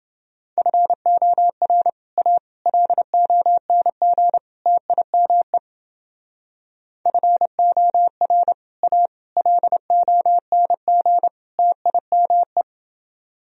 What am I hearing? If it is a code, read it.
Morse code, heard as a keyed tone, about 30 wpm, 700 Hz. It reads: FOR A LONG TIME FOR A LONG TIME